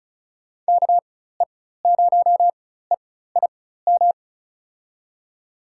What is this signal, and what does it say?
Morse code, heard as a keyed tone, about 35 wpm, spaced as for 16 wpm, 700 Hz. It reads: KE0EIM
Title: USA Call Signs - 4x Character Spacing - 35wpm